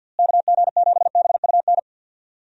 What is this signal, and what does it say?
Morse code, heard as a keyed tone, about 51 wpm, 700 Hz. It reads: KG7BUN
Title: USA Call Signs - 6 Characters - Repeated 2x - 50wpm